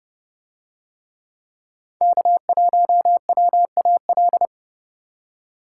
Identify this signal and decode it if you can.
Morse code, heard as a keyed tone, about 29 wpm, 700 Hz. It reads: K1WAL